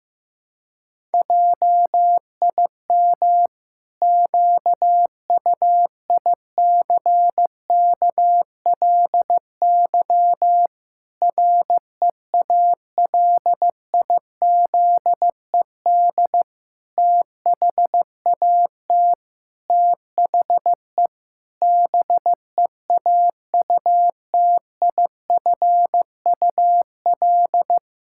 Morse code, heard as a keyed tone, about 15 wpm, 700 Hz. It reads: JIM QUICKLY REALIZED THAT THE BEAUTIFUL
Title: Pangram Challenge v2 - 15wpm